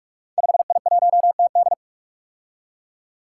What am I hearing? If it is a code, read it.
Morse code, heard as a keyed tone, about 44 wpm, 700 Hz. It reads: 5I1TD
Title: Five Challenging Letter-Number Combinations 45wpm